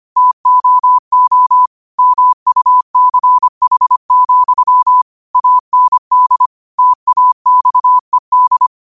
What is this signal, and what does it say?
Morse code, heard as a keyed tone, about 25 wpm, 1000 Hz. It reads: TOO MUCH, AND TAXED
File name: qotdfiles / 25wpm